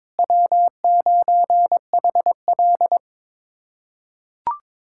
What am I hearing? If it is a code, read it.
Morse code, heard as a keyed tone, about 22 wpm, 700 Hz. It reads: W9HL
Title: USA Callsign DE Callsign - 4 Characters - 22wpm